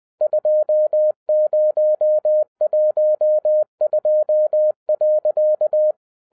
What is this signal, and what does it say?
Morse code, heard as a keyed tone, about 20 wpm, 610 Hz. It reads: 2012.